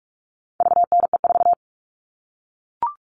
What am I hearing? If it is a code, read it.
Morse code, heard as a keyed tone, about 44 wpm, 700 Hz. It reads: VNE4
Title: All Cut Numbers - 4-Digit 45wpm